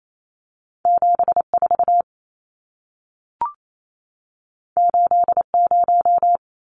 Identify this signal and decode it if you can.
Morse code, heard as a keyed tone, about 28 wpm, 700 Hz. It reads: 74 80
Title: Numbers - 2-Digit 28wpm